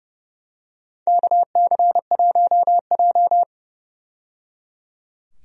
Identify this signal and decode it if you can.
Morse code, heard as a keyed tone, about 30 wpm, 700 Hz. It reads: KC1J
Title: USA Call Signs - 4 Characters - 30wpm